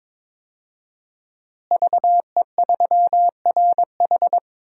Morse code, heard as a keyed tone, about 22 wpm, 700 Hz. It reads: VE3RH
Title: DXCC-100 Call Signs - 5 Characters - Repeated 2x - 22wpm